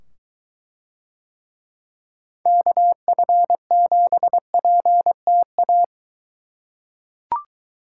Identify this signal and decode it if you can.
Morse code, heard as a keyed tone, about 22 wpm, 700 Hz. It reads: KF7PTA